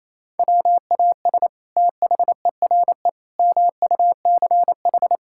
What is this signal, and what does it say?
Morse code, heard as a keyed tone, about 28 wpm, 700 Hz. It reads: WAS THERE MUCH
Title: Sentences from Top 1000 Words - Review Part 5 - 28wpm